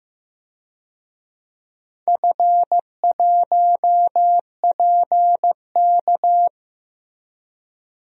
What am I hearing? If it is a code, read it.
Morse code, heard as a keyed tone, about 15 wpm, 700 Hz. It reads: F1PK
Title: DXCC-100 Call Signs - 4 Characters - Repeated 3x - 15wpm